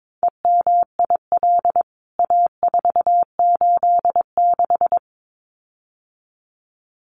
Morse code, heard as a keyed tone, about 22 wpm, 700 Hz. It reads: EMIL A486
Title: CWT Exchange 22wpm